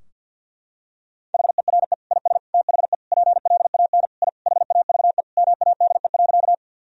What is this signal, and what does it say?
Morse code, heard as a keyed tone, about 49 wpm, 700 Hz. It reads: HERE IS THE PLAN I HAVE MADE.
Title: Sentences from Top 1000 Words - Review Part 2 - 50wpm